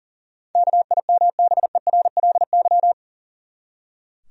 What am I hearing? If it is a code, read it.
Morse code, heard as a keyed tone, about 40 wpm, 700 Hz. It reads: KIMBERLY
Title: USA Common Names - 40wpm